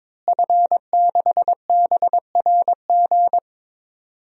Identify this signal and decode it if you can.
Morse code, heard as a keyed tone, about 22 wpm, 700 Hz. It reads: F6BRG
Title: DXCC-100 Call Signs - 5 Characters - Repeated 2x - 22wpm